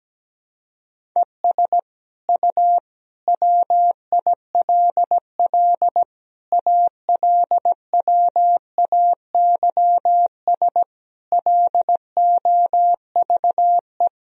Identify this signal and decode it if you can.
Morse code, heard as a keyed tone, about 17 wpm, 700 Hz. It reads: ES U WILL ALWAYS LOVE
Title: Sentences from Top 400 Words and Abbreviations 17wpm